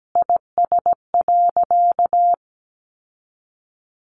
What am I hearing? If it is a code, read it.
Morse code, heard as a keyed tone, about 17 wpm, 700 Hz. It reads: IS.
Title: Sentences from Top 900 Words 17wpm